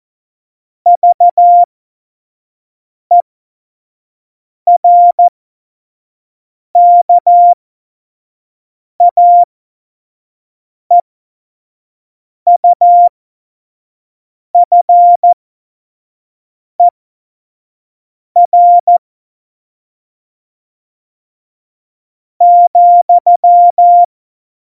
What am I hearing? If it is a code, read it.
Morse code, heard as a keyed tone, about 14 wpm, spaced as for 5 wpm, 700 Hz. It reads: VERKAEUFER,
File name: Morsekurs / Witze / Witz 15. Januar 2019, 25 F70